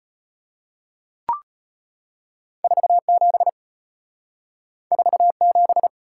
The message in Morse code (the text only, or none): 47 47